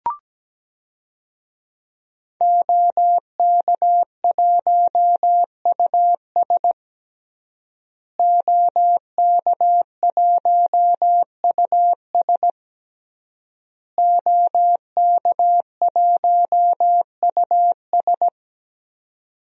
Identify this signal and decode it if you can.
Morse code, heard as a keyed tone, about 17 wpm, 700 Hz. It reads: OK1US OK1US OK1US
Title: DXCC-100 Call Signs - 5 Characters - Repeated 3x - 17wpm